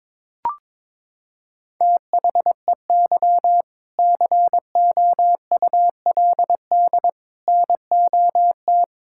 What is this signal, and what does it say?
Morse code, heard as a keyed tone, about 22 wpm, 700 Hz. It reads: THEY COULD NOT
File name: Sentences with 7-Word Phrases - 22wpm